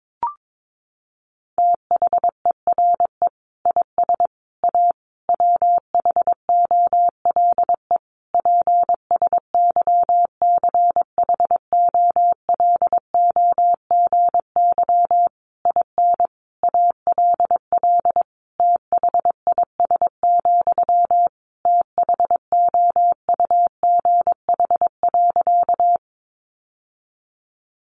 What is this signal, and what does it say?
Morse code, heard as a keyed tone, about 22 wpm, 700 Hz. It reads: THERE IS A WHOLE PSYCHOLOGY IN ALL THIS, THOUGH.
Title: Sentences from Top 1400 Words 22wpm